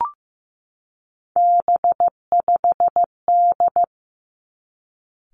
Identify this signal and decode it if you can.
Morse code, heard as a keyed tone, about 15 wpm, 700 Hz. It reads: B5D
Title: Three Challenging Letter-Number Combinations 15wpm